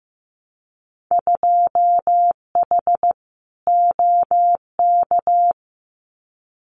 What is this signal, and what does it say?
Morse code, heard as a keyed tone, about 15 wpm, 700 Hz. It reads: 2H OK
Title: Winter Field Day Exchanges Encore 15wpm